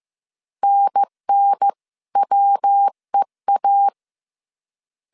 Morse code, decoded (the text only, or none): NNWEA